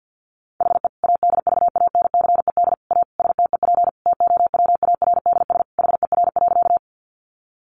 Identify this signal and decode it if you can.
Morse code, heard as a keyed tone, about 50 wpm, 700 Hz. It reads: HE ADVANCED A STEP TOWARDS HER.
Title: Sentences from Top 1200 Words 50wpm